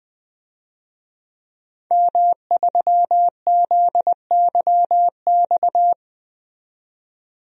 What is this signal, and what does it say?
Morse code, heard as a keyed tone, about 20 wpm, 700 Hz. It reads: M3ZYX
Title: DXCC-100 Call Signs - 5 Characters - 20wpm